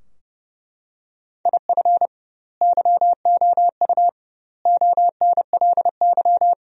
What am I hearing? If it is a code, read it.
Morse code, heard as a keyed tone, about 32 wpm, 700 Hz. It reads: IF YOU ONLY
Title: 2x Word-Spacing - Sentences from Top 400 Words - 30wpm